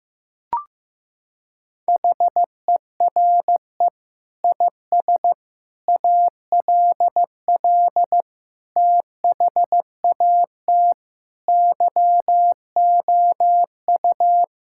Morse code, heard as a keyed tone, about 15 wpm, 700 Hz. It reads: HERE IS ALL THAT YOU
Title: Sentences from Top 1000 Words - Review Part 6 - 15wpm